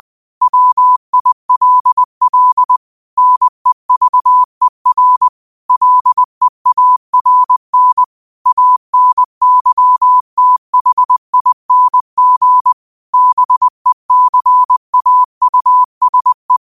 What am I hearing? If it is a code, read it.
Morse code, heard as a keyed tone, about 20 wpm, 1000 Hz. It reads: WILL NEVER LEARN ANYTHING BECAUSE